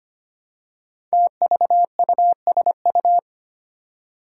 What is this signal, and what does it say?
Morse code, heard as a keyed tone, about 25 wpm, 700 Hz. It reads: TVUSU
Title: Five Challenging Letter-Number Combinations 25wpm